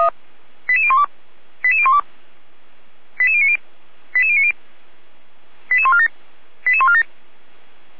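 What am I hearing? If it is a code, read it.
Five-tone selective call (ZVEI), heard as quick runs of five tones, tones of about 70 ms. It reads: 88012 88012 88090 88090 88147 88147
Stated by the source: Five-tone ZVEI selective call received on SDR.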